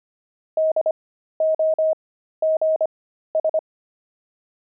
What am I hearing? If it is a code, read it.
Morse code, heard as a keyed tone, about 25 wpm, spaced as for 13 wpm, 630 Hz. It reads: DOGS
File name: pangram-fs-13-cs-25-fox